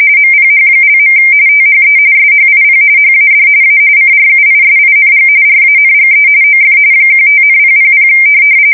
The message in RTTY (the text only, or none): MPH). THE SHIP HAD A CRUISING RANGE OF 15,000